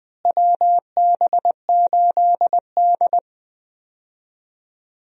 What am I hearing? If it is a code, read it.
Morse code, heard as a keyed tone, about 20 wpm, 700 Hz. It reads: WB8D
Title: USA Call Signs - 4 Characters - Repeated 3x - 20wpm